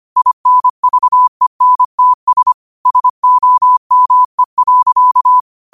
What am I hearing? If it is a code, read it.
Morse code, heard as a keyed tone, about 25 wpm, 1000 Hz. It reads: INVENTS SOME.